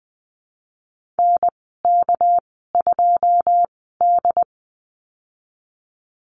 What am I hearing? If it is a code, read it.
Morse code, heard as a keyed tone, about 20 wpm, spaced as for 15 wpm, 700 Hz. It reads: NK2D
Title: 207 Lesson 33g - Letter j - Call Signs 20wpm